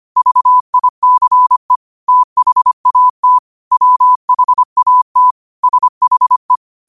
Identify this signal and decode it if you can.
Morse code, heard as a keyed tone, about 25 wpm, 1000 Hz. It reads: VICE THAT WHAT SHE
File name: qotdfiles / 25wpm